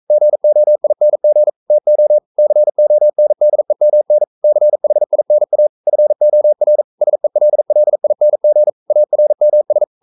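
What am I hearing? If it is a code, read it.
Morse code, heard as a keyed tone, about 42 wpm, 600 Hz. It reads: GOING TO CONDEMN CHINA FOR SELLING ARMS
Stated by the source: Morse code practice at 41 wpm, from CW med Gnister.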